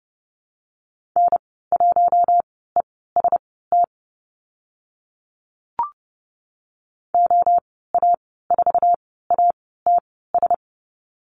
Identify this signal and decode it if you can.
Morse code, heard as a keyed tone, about 30 wpm, spaced as for 17 wpm, 700 Hz. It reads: N1EST OA4ATS
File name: Lesson 14g - Number 2 - Call Signs 30wpm